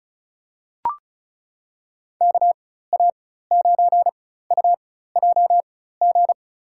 Morse code, heard as a keyed tone, about 34 wpm, spaced as for 16 wpm, 700 Hz. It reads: KA9UJG
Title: USA Call Signs - 6 Characters - 4x Character Spacing - 35wpm